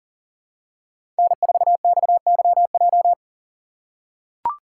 Morse code, heard as a keyed tone, about 40 wpm, 700 Hz. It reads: N4XYJ
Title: USA Call Signs - 5 Characters - 6x Character Spacing - 40wpm